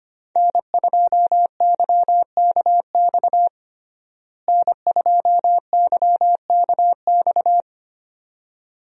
Morse code, heard as a keyed tone, about 25 wpm, 700 Hz. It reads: N2YKX N2YKX
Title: USA Call Signs - 5 Characters - Repeated 2x - 25wpm